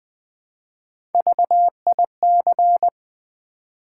Morse code, heard as a keyed tone, about 20 wpm, 700 Hz. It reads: VIC
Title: Build A Word - 1 Letter At A Time - 4-Letter Words - 20wpm